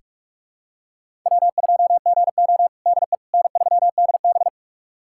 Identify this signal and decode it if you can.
Morse code, heard as a keyed tone, about 46 wpm, 700 Hz. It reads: W2GO DE N3DB
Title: USA Callsign DE Callsign - 4 Characters - 45wpm